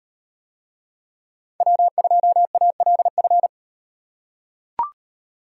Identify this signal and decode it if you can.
Morse code, heard as a keyed tone, about 38 wpm, 700 Hz. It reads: W2ALF